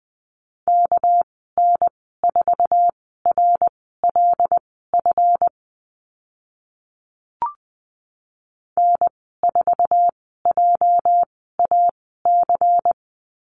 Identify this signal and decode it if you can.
Morse code, heard as a keyed tone, about 20 wpm, spaced as for 14 wpm, 700 Hz. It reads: KN4RLF N4JAC
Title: 231 Lesson 37g - Letter x - Call Signs 20wpm